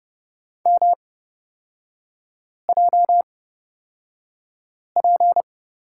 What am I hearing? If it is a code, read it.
Morse code, heard as a keyed tone, about 30 wpm, spaced as for 5 wpm, 700 Hz. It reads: MJP
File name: Lesson 37e - Letter x - Three Characters 30wpm